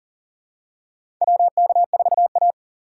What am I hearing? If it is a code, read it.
Morse code, heard as a keyed tone, about 37 wpm, 700 Hz. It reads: WK4A
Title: USA Call Signs - 4 Characters - 4x Character Spacing - 40wpm